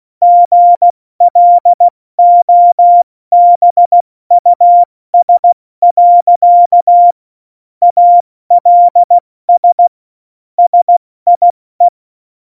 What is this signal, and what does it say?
Morse code, heard as a keyed tone, about 16 wpm, spaced as for 14 wpm, 700 Hz. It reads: GLOBUS. ALS SIE
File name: Morsekurs / Witze / Witz 10.Juli 2018, 70 F80